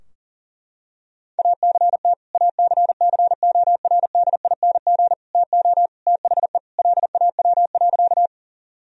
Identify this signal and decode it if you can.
Morse code, heard as a keyed tone, about 40 wpm, 690 Hz. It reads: ACT ACCORDING TO THE LAW.
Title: Sentences from Top 1000 Words - Review Part 4 - 40wpm